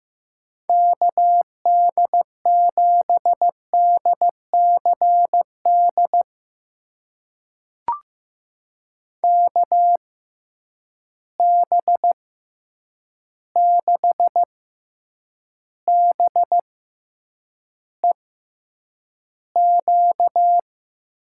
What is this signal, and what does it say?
Morse code, heard as a keyed tone, about 15 wpm, spaced as for 8 wpm, 700 Hz. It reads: KD7DCD KB6BEQ
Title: USA Call Signs - 6 Characters - 6x Character Spacing - 15wpm